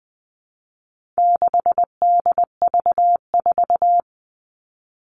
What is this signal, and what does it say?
Morse code, heard as a keyed tone, about 20 wpm, 700 Hz. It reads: 6DV4
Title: All Cut Numbers - 4-Digit 20wpm